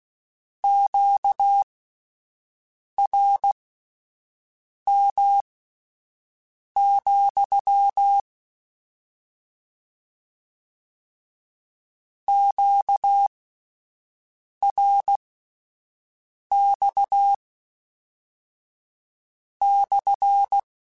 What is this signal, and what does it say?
Morse code, heard as a keyed tone, about 16 wpm, spaced as for 5 wpm, 800 Hz. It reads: QRM, QRX /
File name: Q Calls / Q Calls-mp3s / Q CALLS SPSL @10